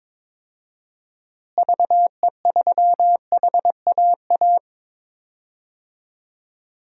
Morse code, heard as a keyed tone, about 21 wpm, 700 Hz. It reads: VE3HAA
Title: DXCC-100 Call Signs - 6 Characters - Repeated 2x - 22wpm